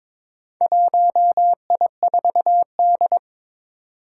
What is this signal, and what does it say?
Morse code, heard as a keyed tone, about 22 wpm, 700 Hz. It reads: 1I4D
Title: Four Challenging Letter-Number Combinations 22wpm